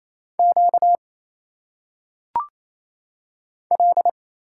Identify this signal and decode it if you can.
Morse code, heard as a keyed tone, about 27 wpm, 700 Hz. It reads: Q L
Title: Single Letters - Phonetic Pronunciation 28wpm